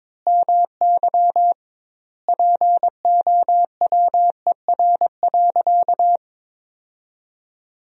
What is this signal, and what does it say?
Morse code, heard as a keyed tone, about 22 wpm, 700 Hz. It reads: MY POWER.
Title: 2x Word-Spacing - Sentences from Top 500 Words - 22wpm